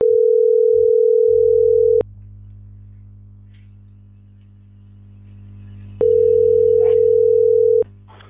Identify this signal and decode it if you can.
Ringback tone (North America), 440 + 480 Hz.